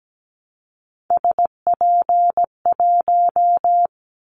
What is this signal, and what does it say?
Morse code, heard as a keyed tone, about 17 wpm, 700 Hz. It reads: SP1
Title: Callsign Prefixes for Top 25 DXCC - Repeated 3x - 17wpm